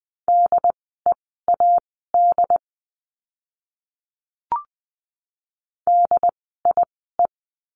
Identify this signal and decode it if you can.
Morse code, heard as a keyed tone, about 20 wpm, spaced as for 14 wpm, 700 Hz. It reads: DEAD DIE
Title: 075 Lesson 13f - Letter L - Words 20wpm